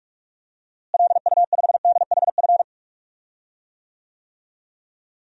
Morse code, heard as a keyed tone, about 46 wpm, 700 Hz. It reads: LU5DHF
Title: DXCC-100 Call Signs - 6 Characters - Repeated 3x - 45wpm